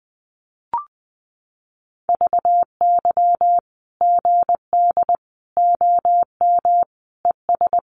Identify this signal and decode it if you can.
Morse code, heard as a keyed tone, about 19 wpm, 700 Hz. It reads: VY GD OM ES
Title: Common QSO Phrases 20wpm